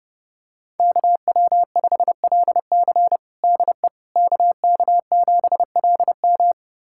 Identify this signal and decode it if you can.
Morse code, heard as a keyed tone, about 30 wpm, 700 Hz. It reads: KW5LC DE KK7LM
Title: USA Callsign DE Callsign - 5 Characters - 30wpm